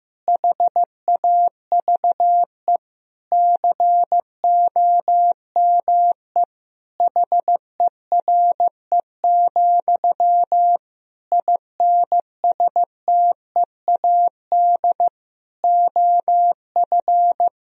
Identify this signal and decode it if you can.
Morse code, heard as a keyed tone, about 16 wpm, 700 Hz. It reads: HAVE COME HERE, INSTEAD OF
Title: Sentences from Top 1000 Words - Review Part 4 - 15wpm